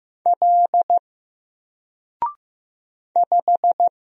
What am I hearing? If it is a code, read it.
Morse code, heard as a keyed tone, about 15 wpm, 700 Hz. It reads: L 5
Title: Single Letter-Number Alt 15wpm